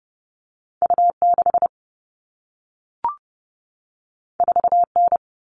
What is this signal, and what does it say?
Morse code, heard as a keyed tone, about 30 wpm, 700 Hz. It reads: U6 4N
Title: All Cut Numbers - 2-Digit 30wpm